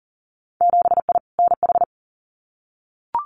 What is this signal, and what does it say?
Morse code, heard as a keyed tone, about 41 wpm, 700 Hz. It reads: 7I NH
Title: Winter Field Day Exchanges Encore 40wpm